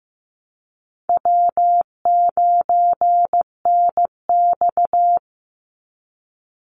Morse code, heard as a keyed tone, about 15 wpm, 700 Hz. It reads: W9NX
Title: Easy Call Signs Repeated 3x - 15wpm